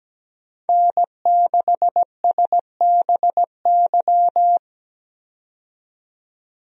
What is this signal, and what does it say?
Morse code, heard as a keyed tone, about 17 wpm, 700 Hz. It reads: N6SBY
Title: USA Call Signs - 5 Characters - 17wpm